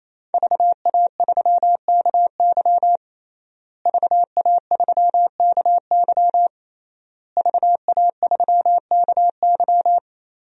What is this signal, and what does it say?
Morse code, heard as a keyed tone, about 28 wpm, 700 Hz. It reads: VA3KY VA3KY VA3KY
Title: DXCC-100 Call Signs - 5 Characters - Repeated 3x - 28wpm